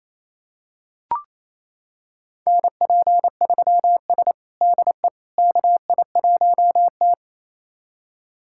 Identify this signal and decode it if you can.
Morse code, heard as a keyed tone, about 27 wpm, 700 Hz. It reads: NP3S DE KI1T
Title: USA Callsign DE Callsign - 4 Characters - 28wpm